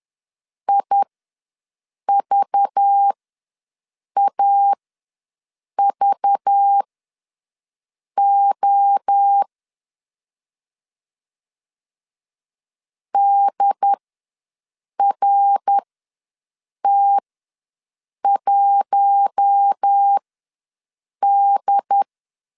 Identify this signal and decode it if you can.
Morse code, heard as a keyed tone, about 10 wpm, spaced as for 6 wpm, 800 Hz. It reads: IVAVO DRT1D
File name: LEZIONI / 5 LBVC / 5 LBV ws 10wpm 3645